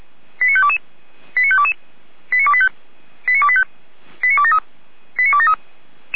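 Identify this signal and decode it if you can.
Five-tone selective call (ZVEI), heard as quick runs of five tones, tones of about 70 ms. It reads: 89522 89522 89285 89285 89273 89273
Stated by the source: ZVEI five-tone selective call received on SDR, fire brigade alerting.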